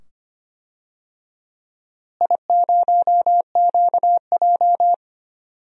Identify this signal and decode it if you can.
Morse code, heard as a keyed tone, about 25 wpm, 700 Hz. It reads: I0QJ
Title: DXCC-100 Call Signs - 4 Characters - Repeated 2x - 25wpm